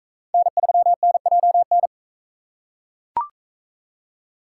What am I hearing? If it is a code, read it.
Morse code, heard as a keyed tone, about 41 wpm, 700 Hz. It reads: N3NJN